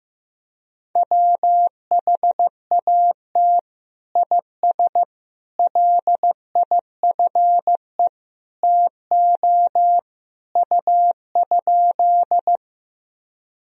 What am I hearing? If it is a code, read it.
Morse code, heard as a keyed tone, about 16 wpm, 700 Hz. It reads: WHAT IS LIFE TO U?
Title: Sentences from Top 300 Words and Abbreviations 15wpm